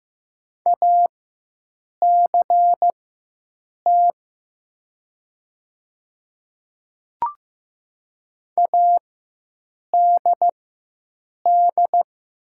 Morse code, heard as a keyed tone, about 15 wpm, spaced as for 7 wpm, 700 Hz. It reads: ACT ADD
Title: 4x Character Spacing - Top 300 Words - 15wpm